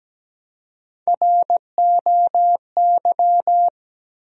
Morse code, heard as a keyed tone, about 17 wpm, 700 Hz. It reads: ROY
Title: USA Common Names - Max 3 Letters - 17wpm